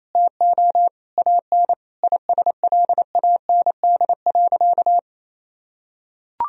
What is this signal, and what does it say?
Morse code, heard as a keyed tone, about 27 wpm, 700 Hz. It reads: TO AN ISLAND.
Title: Sentences from Top 1000 Words - Review Part 2 - 28wpm